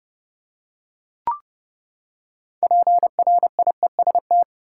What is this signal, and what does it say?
Morse code, heard as a keyed tone, about 29 wpm, 700 Hz. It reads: PRIEST
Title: Six Letter Words - 30wpm